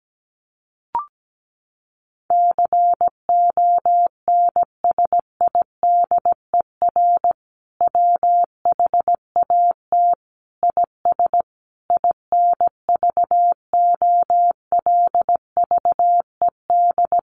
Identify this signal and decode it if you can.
Morse code, heard as a keyed tone, about 17 wpm, 700 Hz. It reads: CONSIDER WHAT IS INVOLVED
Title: Sentences from Top 800 Words 17wpm